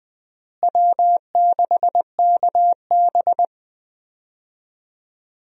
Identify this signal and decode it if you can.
Morse code, heard as a keyed tone, about 20 wpm, 700 Hz. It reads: W6KB
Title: USA Call Signs - 4 Characters - Repeated 3x - 20wpm